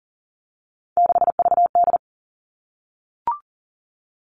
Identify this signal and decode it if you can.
Morse code, heard as a keyed tone, about 40 wpm, 700 Hz. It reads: 6VD
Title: All Cut Numbers - 3-Digit 40wpm